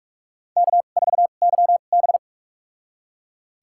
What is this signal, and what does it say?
Morse code, heard as a keyed tone, about 41 wpm, spaced as for 31 wpm, 700 Hz. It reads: K4YB
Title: USA Call Signs - 2x Character Spacing - 45wpm